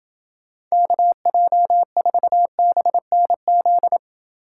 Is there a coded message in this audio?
Yes: Morse code from 0.7 to 4.0 s, heard as a keyed tone, about 29 wpm, 700 Hz. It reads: KJ4BNZ